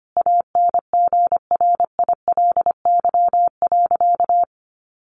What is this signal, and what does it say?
Morse code, heard as a keyed tone, about 25 wpm, 700 Hz. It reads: ANGRILY.
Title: A Sound Of Thunder 25wpm